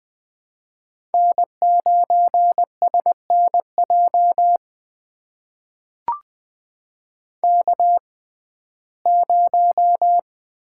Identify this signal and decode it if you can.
Morse code, heard as a keyed tone, about 21 wpm, spaced as for 15 wpm, 700 Hz. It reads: N9SNJ K0
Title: USA Call Signs - 5 Characters - 6x Character Spacing - 20wpm